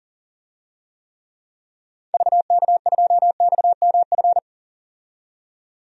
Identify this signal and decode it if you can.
Morse code, heard as a keyed tone, about 40 wpm, 700 Hz. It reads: VK2XMF
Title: DXCC-100 Call Signs - 6 Characters - 40wpm